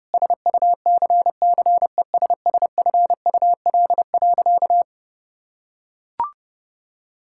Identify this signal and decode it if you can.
Morse code, heard as a keyed tone, about 29 wpm, 700 Hz. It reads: SUCCESSFUL.
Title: Sentences from Top 1000 Words - Review Part 6 - 30wpm